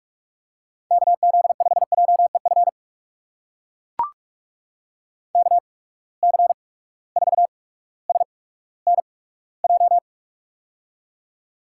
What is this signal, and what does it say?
Morse code, heard as a keyed tone, about 43 wpm, spaced as for 19 wpm, 700 Hz. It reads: KZ5JEF KC4SNJ
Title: USA Call Signs - 6 Characters - 8x Character Spacing - 45wpm